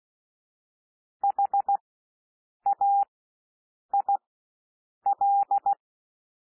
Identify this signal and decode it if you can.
Morse code, heard as a keyed tone, about 16 wpm, spaced as for 7 wpm, 800 Hz. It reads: HAIL
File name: cwmp3 / W1 SOUND WORDS 16 wpm